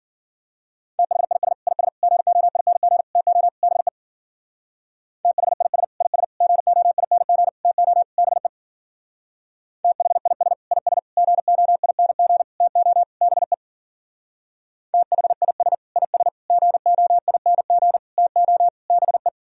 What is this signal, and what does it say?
Morse code, heard as a keyed tone, about 53 wpm, 700 Hz. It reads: THIS IS GOING TO BE THIS IS GOING TO BE THIS IS GOING TO BE THIS IS GOING TO BE